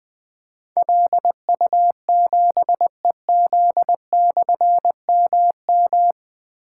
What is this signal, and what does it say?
Morse code, heard as a keyed tone, about 20 wpm, 700 Hz. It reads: LU7EZ/MM
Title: Difficult Call Signs 20wpm